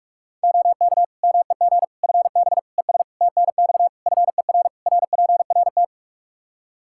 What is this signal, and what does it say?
Morse code, heard as a keyed tone, about 46 wpm, 700 Hz. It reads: OK MEG FB ES TNX FER RPRT
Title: Practice QSO - Exchange 2 - Info - 45wpm